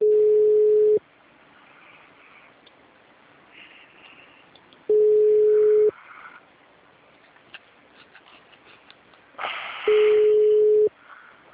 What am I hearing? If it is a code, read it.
Ringback tone (Europe, 425 Hz).
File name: out- -2017- - - .18568